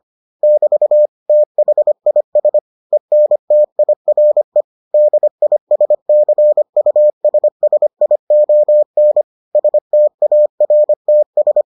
Morse code, heard as a keyed tone, about 25 wpm, 600 Hz. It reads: <BT> THIS ENTIRE DISCUSSION STARTS